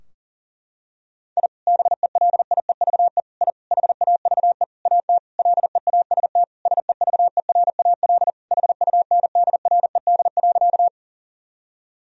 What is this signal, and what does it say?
Morse code, heard as a keyed tone, about 40 wpm, 700 Hz. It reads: I BELIEVE I HAVE AT LEAST SEVERAL HUNDRED.
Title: Sentences from Top 1000 Words - Review Part 2 - 40wpm